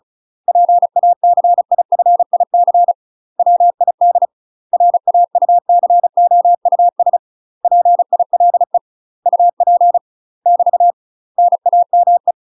Morse code, heard as a keyed tone, about 36 wpm, 700 Hz. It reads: PACIFIC WID RAUCOUS PILE UP <BT> NAME